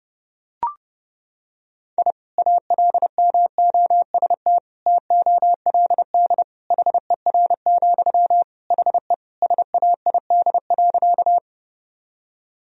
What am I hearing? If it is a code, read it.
Morse code, heard as a keyed tone, about 30 wpm, 700 Hz. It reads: I ALMOST TOLD HER, HE SAID.
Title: Sentences from Top 1000 Words - Review Part 4 - 30wpm